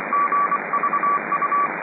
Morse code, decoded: GVU